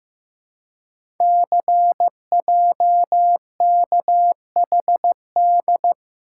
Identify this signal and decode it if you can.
Morse code, heard as a keyed tone, about 15 wpm, 700 Hz. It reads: CJKHD
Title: Code Groups - 5 Letter:Numbers - 15wpm